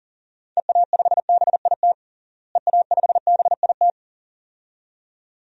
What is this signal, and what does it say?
Morse code, heard as a keyed tone, about 37 wpm, 700 Hz. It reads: EA5BIT EA5BIT
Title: DXCC-100 Call Signs - 6 Characters - Repeated 2x - 40wpm